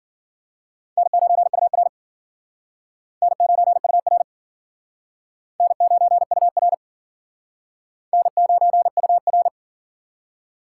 Morse code, heard as a keyed tone, about 51 wpm, 700 Hz. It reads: N9UR N9UR N9UR N9UR